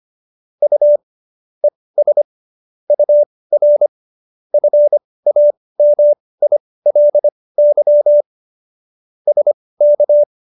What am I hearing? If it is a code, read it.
Morse code, heard as a keyed tone, about 25 wpm, spaced as for 18 wpm, 590 Hz. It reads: U ES UR FAMILY SK